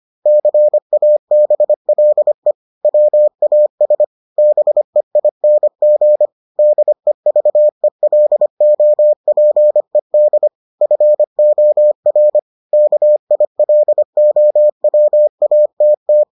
Morse code, heard as a keyed tone, about 25 wpm, 600 Hz. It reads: CABLE WAS BEING DEVELOPED FOR KILOWATT